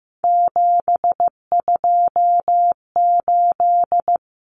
Morse code, heard as a keyed tone, about 15 wpm, 700 Hz. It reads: 728
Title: Common Cut Numbers - 3-Digit 15wpm